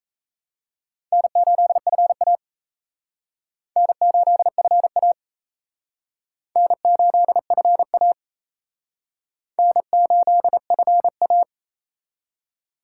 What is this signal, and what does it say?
Morse code, heard as a keyed tone, about 34 wpm, 700 Hz. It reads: N8FA N8FA N8FA N8FA
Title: Speed Racing - USA 1x2 Call Signs 28wpm